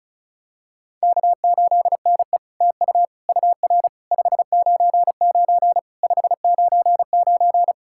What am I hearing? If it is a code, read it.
Morse code, heard as a keyed tone, about 36 wpm, 700 Hz. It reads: K8NE TU UR 599 599
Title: SOTA QSO Practice 35wpm